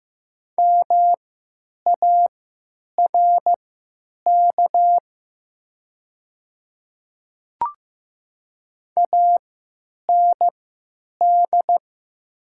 Morse code, heard as a keyed tone, about 15 wpm, spaced as for 8 wpm, 700 Hz. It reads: MARK AND
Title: 3x Character Spacing - Top 500 Words - 15wpm